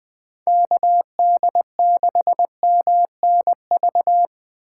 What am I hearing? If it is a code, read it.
Morse code, heard as a keyed tone, about 20 wpm, 700 Hz. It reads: KD6MNV
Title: USA Call Signs - 6 Characters - 4x Character Spacing - 20wpm